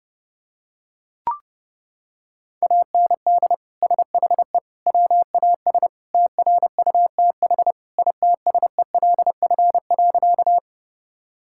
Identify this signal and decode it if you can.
Morse code, heard as a keyed tone, about 29 wpm, 700 Hz. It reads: AND SHE WAS TRUTH ITSELF.
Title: Sentences from Top 1000 Words - Review Part 5 - 30wpm